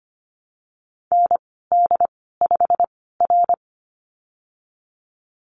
Morse code, heard as a keyed tone, about 25 wpm, spaced as for 16 wpm, 700 Hz. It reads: ND5R
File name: Lesson 15g - Number 5 - Call Signs 25wpm